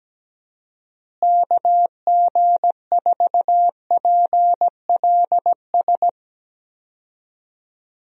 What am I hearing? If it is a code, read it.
Morse code, heard as a keyed tone, about 17 wpm, 700 Hz. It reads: KG4PLS
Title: USA Call Signs - 6 Characters - Repeated 3x - 17wpm